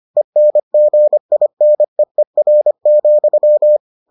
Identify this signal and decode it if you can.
Morse code, heard as a keyed tone, about 25 wpm, 600 Hz. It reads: ENGINEER,